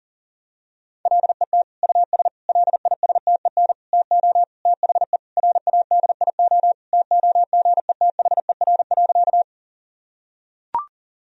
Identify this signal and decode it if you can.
Morse code, heard as a keyed tone, about 37 wpm, 700 Hz. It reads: LET US LISTEN TO THE RADIO TOGETHER.
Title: Sentences from Top 1000 Words - Review Part 3 - 40wpm